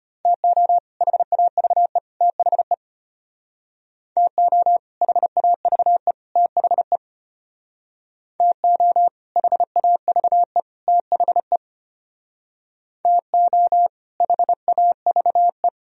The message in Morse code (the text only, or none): TO HAVE THE TO HAVE THE TO HAVE THE TO HAVE